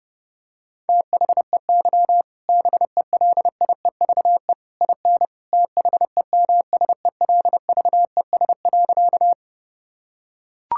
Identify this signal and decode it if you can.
Morse code, heard as a keyed tone, about 32 wpm, 700 Hz. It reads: THEY BELIEVE IN THEMSELVES.
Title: Sentences from Top 1000 Words - Review Part 5 - 30wpm